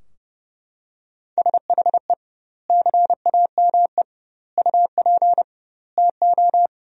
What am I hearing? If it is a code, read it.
Morse code, heard as a keyed tone, about 32 wpm, 700 Hz. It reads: SHE CAME UP TO
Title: 2x Word-Spacing - Sentences from Top 200 Words - 30wpm